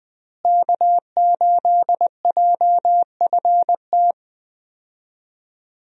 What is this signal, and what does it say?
Morse code, heard as a keyed tone, about 21 wpm, 700 Hz. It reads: K8JFT
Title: USA Call Signs - 5 Characters - Repeated 2x - 20wpm